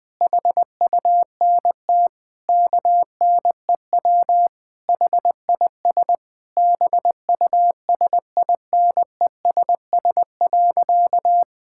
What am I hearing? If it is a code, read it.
Morse code, heard as a keyed tone, about 20 wpm, 700 Hz. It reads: HUNT KNEW HIS BUSINESS.
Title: Sentences from Top 1600 Words 20wpm